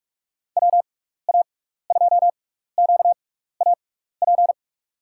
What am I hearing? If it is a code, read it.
Morse code, heard as a keyed tone, about 45 wpm, spaced as for 15 wpm, 700 Hz. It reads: WA2QAP
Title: USA Call Signs - 6 Characters - 6x Character Spacing - 45wpm